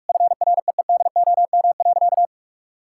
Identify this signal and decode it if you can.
Morse code, heard as a keyed tone, about 42 wpm, 700 Hz. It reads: FREEDOM.
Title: Sentences from Top 1000 Words - Review Part 5 - 45wpm